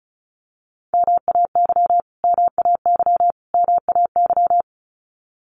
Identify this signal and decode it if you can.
Morse code, heard as a keyed tone, about 34 wpm, 700 Hz. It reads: MAY MAY MAY
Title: Top 500 Words Repeated 35wpm